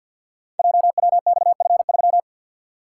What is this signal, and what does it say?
Morse code, heard as a keyed tone, about 47 wpm, 700 Hz. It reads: JWXF3
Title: Code Groups - 5 Letter:Numbers - 50wpm